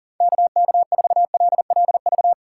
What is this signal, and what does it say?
Morse code, heard as a keyed tone, about 40 wpm, 700 Hz. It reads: KK4LLV
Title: USA Call Signs - 4x Character Spacing - 40wpm